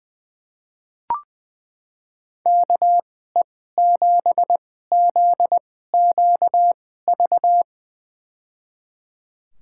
Morse code, heard as a keyed tone, about 20 wpm, spaced as for 14 wpm, 700 Hz. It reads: KE7ZQV
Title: USA Call Signs - 6 Characters - 2x Character Spacing - 20wpm